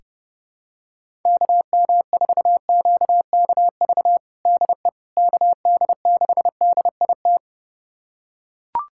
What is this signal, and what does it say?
Morse code, heard as a keyed tone, about 30 wpm, 700 Hz. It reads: KM4QKV DE KD6DIT
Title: USA Callsign DE Callsign - 6 Characters - 30wpm